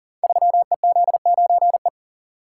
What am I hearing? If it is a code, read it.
Morse code, heard as a keyed tone, about 41 wpm, 700 Hz. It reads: 3EZ9E
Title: Code Groups - 5 Letter:Numbers - 40wpm